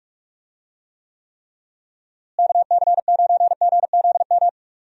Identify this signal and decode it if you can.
Morse code, heard as a keyed tone, about 46 wpm, 700 Hz. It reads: KC9GZM
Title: USA Call Signs - 6 Characters - Repeated 3x - 45wpm